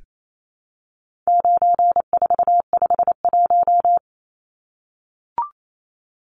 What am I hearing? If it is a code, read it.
Morse code, heard as a keyed tone, about 28 wpm, 700 Hz. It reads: 9451
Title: Numbers - 4-Digit 28wpm